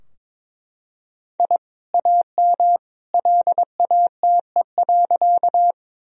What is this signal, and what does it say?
Morse code, heard as a keyed tone, about 22 wpm, 700 Hz. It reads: I AM LATE.
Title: Sentences from Top 1000 Words - Review Part 2 - 22wpm